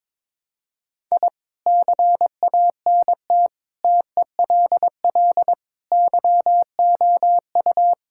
Morse code, heard as a keyed tone, about 22 wpm, 700 Hz. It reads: I CANT TELL YOU
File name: practice / 22wpm / Sets of 6 Words - 22wpm